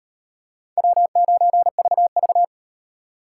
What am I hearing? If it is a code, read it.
Morse code, heard as a keyed tone, about 36 wpm, 700 Hz. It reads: W9VV